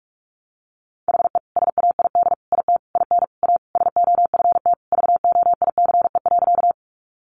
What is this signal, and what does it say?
Morse code, heard as a keyed tone, about 44 wpm, 700 Hz. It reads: HE SAID IT IN A SOFT VOICE.
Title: Sentences from Top 1500 Words 45wpm